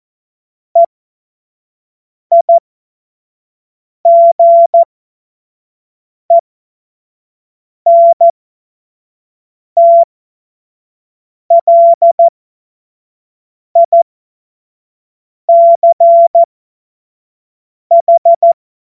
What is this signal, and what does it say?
Morse code, heard as a keyed tone, about 14 wpm, spaced as for 5 wpm, 680 Hz. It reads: EIGENTLICH
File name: Morsekurs / Witze / Witz 28. Dezember 2021, 25 F70